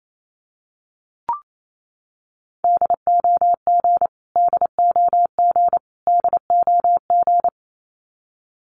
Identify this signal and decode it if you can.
Morse code, heard as a keyed tone, about 28 wpm, 700 Hz. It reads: DOG DOG DOG
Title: Top 500 Words Repeated 28wpm